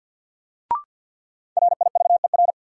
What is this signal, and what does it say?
Morse code, heard as a keyed tone, about 51 wpm, 700 Hz. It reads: RIVER
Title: Sentences from Top 300 Words - New Word Drill - 50wpm